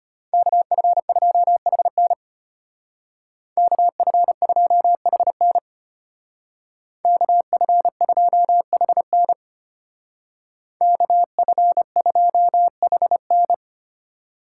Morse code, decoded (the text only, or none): KF2HN KF2HN KF2HN KF2HN